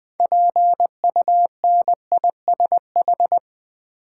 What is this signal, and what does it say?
Morse code, heard as a keyed tone, about 20 wpm, 690 Hz. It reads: PUNISH
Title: 6 Letter Words - First 3 Letters Given - 20wpm